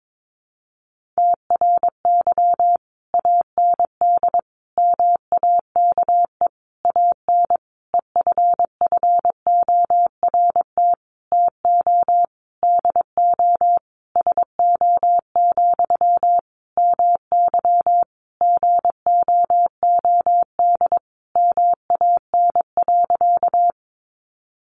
Morse code, heard as a keyed tone, about 22 wpm, 700 Hz. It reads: TRY AND MAKE AN EFFORT TO DO SO, MY GOOD MAN.
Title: Sentences from Top 1200 Words 22wpm